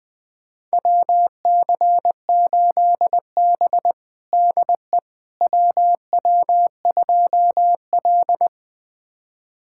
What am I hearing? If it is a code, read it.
Morse code, heard as a keyed tone, about 20 wpm, 700 Hz. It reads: WC8B DE WW2L
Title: USA Callsign DE Callsign - 4 Characters - 20wpm